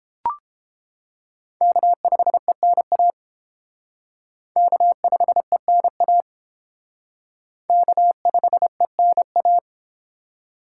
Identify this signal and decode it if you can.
Morse code, heard as a keyed tone, about 28 wpm, 700 Hz. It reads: K5ENA K5ENA K5ENA